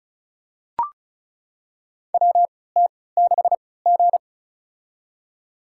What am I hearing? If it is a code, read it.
Morse code, heard as a keyed tone, about 36 wpm, spaced as for 20 wpm, 700 Hz. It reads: WT6G
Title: USA Call Signs - 4 Characters - 3x Character Spacing - 35wpm